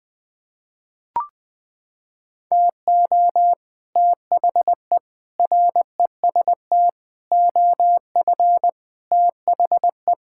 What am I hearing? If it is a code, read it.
Morse code, heard as a keyed tone, about 20 wpm, 700 Hz. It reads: TO THE REST OF THE
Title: Sets of 6 Words - 20wpm